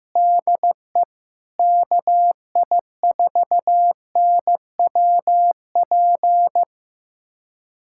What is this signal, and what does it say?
Morse code, heard as a keyed tone, about 15 wpm, 700 Hz. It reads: DE KI4NWP
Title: USA Callsign DE Callsign 15wpm